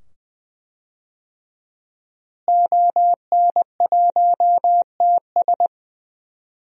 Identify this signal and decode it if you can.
Morse code, heard as a keyed tone, about 20 wpm, 700 Hz. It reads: ON1TS
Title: DXCC-100 Call Signs - 5 Characters - Repeated 2x - 20wpm